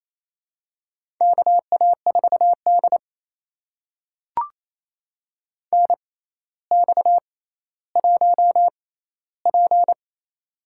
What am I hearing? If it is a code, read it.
Morse code, heard as a keyed tone, about 27 wpm, spaced as for 14 wpm, 700 Hz. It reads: KA4D NX1P
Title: USA Call Signs - 4 Characters - 6x Character Spacing - 28wpm